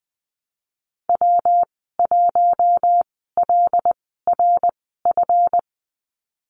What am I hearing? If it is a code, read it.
Morse code, heard as a keyed tone, about 20 wpm, spaced as for 15 wpm, 700 Hz. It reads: W1LRF
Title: 231 Lesson 37g - Letter x - Call Signs 20wpm